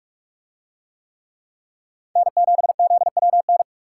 Morse code, heard as a keyed tone, about 46 wpm, 700 Hz. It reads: N7ZWN